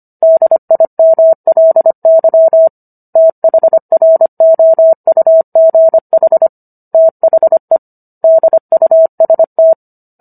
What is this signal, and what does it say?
Morse code, heard as a keyed tone, about 25 wpm, 650 Hz. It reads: DIMLY THROUGH THE DUST